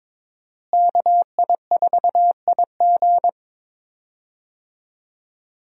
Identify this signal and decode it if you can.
Morse code, heard as a keyed tone, about 22 wpm, 700 Hz. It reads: KI4IG
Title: USA Call Signs - 5 Characters - Repeated 3x - 22wpm